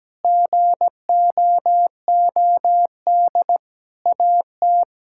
Morse code, heard as a keyed tone, about 17 wpm, 700 Hz. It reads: GOOD AT
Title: Sentences from Top 1000 Words - Review Part 6 - 17wpm